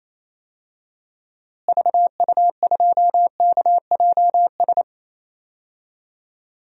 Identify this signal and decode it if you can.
Morse code, heard as a keyed tone, about 28 wpm, 700 Hz. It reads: VU2KJS
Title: DXCC-100 Call Signs - 6 Characters - Repeated 3x - 28wpm